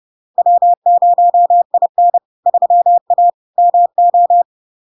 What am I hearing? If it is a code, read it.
Morse code, heard as a keyed tone, about 30 wpm, 700 Hz. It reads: W0IN 3A MO
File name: LARK-FD-2022-Log-in-CW-30wpm-700-Hz